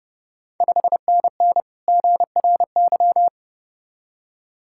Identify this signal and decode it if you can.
Morse code, heard as a keyed tone, about 29 wpm, 690 Hz. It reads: 5NN GRY
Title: Ontario QSO Party Exchange 30wpm